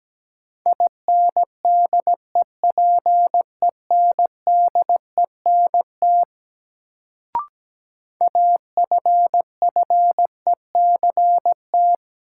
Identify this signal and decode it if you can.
Morse code, heard as a keyed tone, about 17 wpm, 700 Hz. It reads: INDEPENDENT AFFECT
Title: Sentences from Top 900 Words - New Word Drill - 17wpm